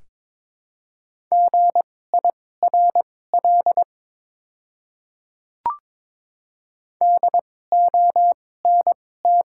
Morse code, heard as a keyed tone, about 22 wpm, spaced as for 16 wpm, 690 Hz. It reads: GIRL DONT
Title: 2x Character Spacing - Top 500 Words - 22wpm